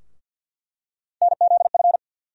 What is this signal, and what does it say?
Morse code, heard as a keyed tone, about 53 wpm, 700 Hz. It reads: NZF
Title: USA Call Sign Postfixes - 3 Letters 50wpm